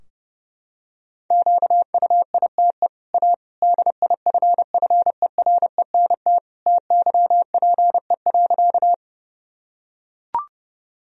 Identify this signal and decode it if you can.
Morse code, heard as a keyed tone, about 30 wpm, 700 Hz. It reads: QUITE A DIFFERENT TYPE.
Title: Sentences from Top 1000 Words - Review Part 3 - 30wpm